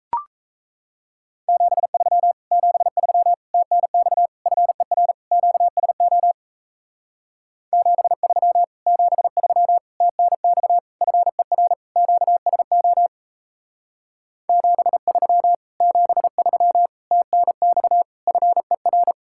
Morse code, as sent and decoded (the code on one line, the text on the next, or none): --... ...-- / --... ...-- / - -. -..- / ..-. .-. / --.- ... --- / --... ...-- / --... ...-- / - -. -..- / ..-. .-. / --.- ... --- / --... ...-- / --... ...-- / - -. -..- / ..-. .-.
73 73 TNX FER QSO 73 73 TNX FER QSO 73 73 TNX FER